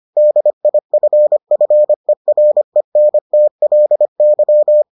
Morse code, heard as a keyed tone, about 25 wpm, 600 Hz. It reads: DIFFERENTLY